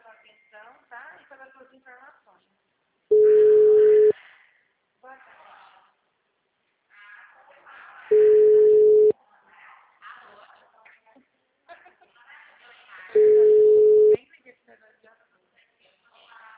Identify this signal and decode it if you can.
Ringback tone (Europe, 425 Hz), with speech.